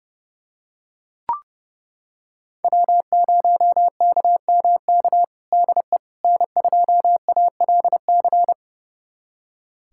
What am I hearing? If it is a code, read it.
Morse code, heard as a keyed tone, about 30 wpm, 700 Hz. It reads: W0KMK DE N2ALC
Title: USA Callsign DE Callsign - 5 Characters - 30wpm